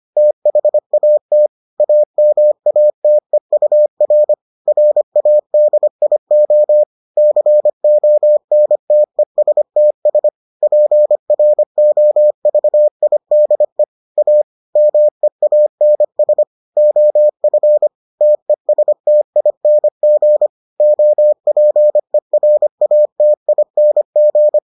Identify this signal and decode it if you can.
Morse code, heard as a keyed tone, about 25 wpm, 600 Hz. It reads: THAT AMATEUR RADIO CONTESTS PROVIDE A MEANS OF TESTING OPERATING